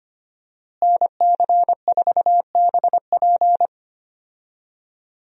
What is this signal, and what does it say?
Morse code, heard as a keyed tone, about 25 wpm, 700 Hz. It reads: NC4BP
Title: USA Call Signs - 5 Characters - Repeated 2x - 25wpm